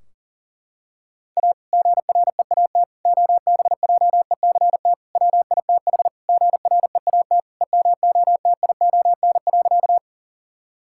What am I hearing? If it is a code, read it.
Morse code, heard as a keyed tone, about 40 wpm, 700 Hz. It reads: A GREAT OBJECT WITH GREAT EMOTION.
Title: Sentences from Top 1600 Words 40wpm